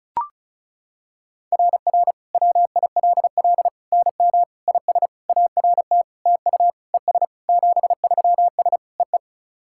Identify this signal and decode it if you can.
Morse code, heard as a keyed tone, about 36 wpm, 700 Hz. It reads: RR WILL NM IS ART TU ES 73S EE
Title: SOTA QSO Practice 35wpm